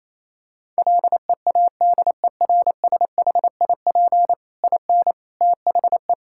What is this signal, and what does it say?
Morse code, heard as a keyed tone, about 28 wpm, 700 Hz. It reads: LEADERSHIP IN THE
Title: Sentences from Top 1600 Words 28wpm